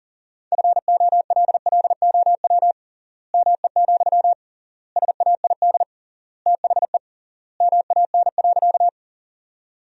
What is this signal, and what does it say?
Morse code, heard as a keyed tone, about 40 wpm, 700 Hz. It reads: FOLLOW ME, SAID THE MAN.
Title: 3x Word-Spacing - Sentences from Top 200 Words - 40wpm